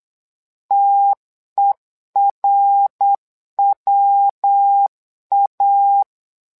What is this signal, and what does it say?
Morse code, heard as a keyed tone, about 8 wpm, 800 Hz. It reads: TERWA
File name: LEZIONI / 3 MWRO / 3 MWRO 8wpm 2606